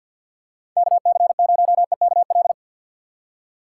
Morse code, heard as a keyed tone, about 52 wpm, 700 Hz. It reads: KC0EKL
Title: USA Call Signs - 6 Characters - 6x Character Spacing - 50wpm